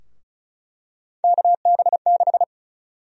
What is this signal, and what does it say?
Morse code, heard as a keyed tone, about 36 wpm, 700 Hz. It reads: KB6
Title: USA Call Sign Prefixes - Repeated 2x - 35wpm